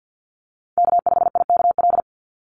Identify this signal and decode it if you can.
Morse code, heard as a keyed tone, about 48 wpm, 700 Hz. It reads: K5IKL
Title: USA Call Signs 50wpm